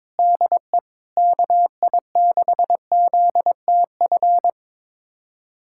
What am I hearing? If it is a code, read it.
Morse code, heard as a keyed tone, about 22 wpm, 700 Hz. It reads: DE KI6ZTF
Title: USA Callsign DE Callsign - 6 Characters - 22wpm